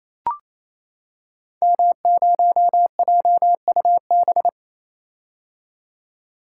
Morse code, heard as a keyed tone, about 27 wpm, 700 Hz. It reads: M0JUB
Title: Code Groups - 5 Letter:Numbers - 28wpm